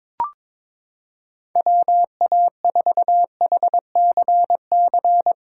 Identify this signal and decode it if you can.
Morse code, heard as a keyed tone, about 22 wpm, 700 Hz. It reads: WA4HCC
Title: K1USN SST QSO Practice - 22wpm